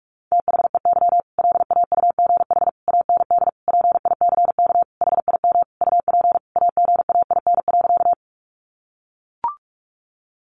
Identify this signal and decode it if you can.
Morse code, heard as a keyed tone, about 44 wpm, 700 Hz. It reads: THEY LAUGH AND PICK HIM UP AGAIN.
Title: Sentences from Top 1000 Words 45wpm